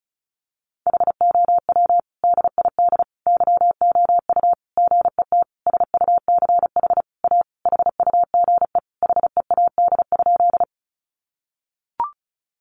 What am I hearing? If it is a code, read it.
Morse code, heard as a keyed tone, about 34 wpm, 700 Hz. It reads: HOW DID YOU GET SUCH A HUGE HEAD?
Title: Sentences from Top 1100 Words 35wpm